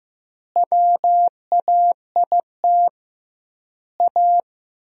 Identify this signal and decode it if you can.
Morse code, heard as a keyed tone, about 15 wpm, 700 Hz. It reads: WAIT A
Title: 2x Word-Spacing - Sentences from Top 500 Words - 15wpm